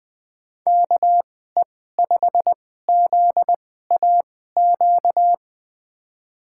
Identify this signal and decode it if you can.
Morse code, heard as a keyed tone, about 20 wpm, spaced as for 14 wpm, 700 Hz. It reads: KE5ZAQ
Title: USA Call Signs - 6 Characters - 2x Character Spacing - 20wpm